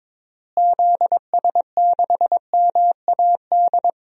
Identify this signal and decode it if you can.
Morse code, heard as a keyed tone, about 22 wpm, 700 Hz. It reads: ZS6MAD
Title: DXCC-100 Call Signs - 6 Characters - Repeated 2x - 22wpm